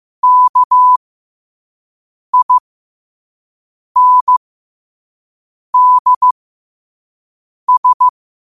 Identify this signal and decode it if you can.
Morse code, heard as a keyed tone, about 15 wpm, spaced as for 5 wpm, 1000 Hz. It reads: KINDS
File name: qotdfiles / 5wpm